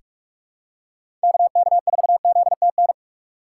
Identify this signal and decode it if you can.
Morse code, heard as a keyed tone, about 46 wpm, 700 Hz. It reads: KK4ZTN
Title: USA Call Signs - 6 Characters - 2x Character Spacing - 45wpm